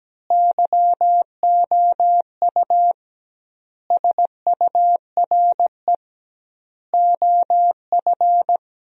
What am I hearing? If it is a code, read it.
Morse code, heard as a keyed tone, about 17 wpm, 700 Hz. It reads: YOU SURE OF
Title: 2x Word-Spacing - Sentences from Top 300 Words - 17wpm